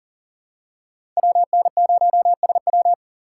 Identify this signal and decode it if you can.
Morse code, heard as a keyed tone, about 39 wpm, 700 Hz. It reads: WN0SW
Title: USA Call Signs - 6x Character Spacing - 40wpm